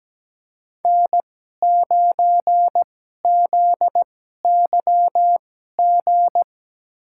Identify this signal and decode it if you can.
Morse code, heard as a keyed tone, about 17 wpm, spaced as for 12 wpm, 700 Hz. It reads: N9ZYG
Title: USA Call Signs - 2x Character Spacing - 17wpm